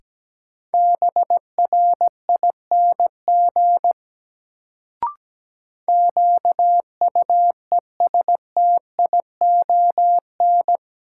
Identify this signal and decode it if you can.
Morse code, heard as a keyed tone, about 17 wpm, 700 Hz. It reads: BRING QUESTION
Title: Sets of 5 Words Encore - New Word Drill - 17wpm